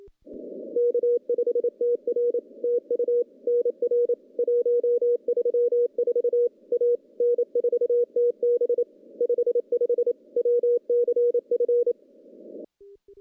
Morse code, heard as a keyed tone, about 28 wpm, 480 Hz. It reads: K5TR TU NR 134 A N4TB 55 WCF
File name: contests / CW-SS-k5tr- -0800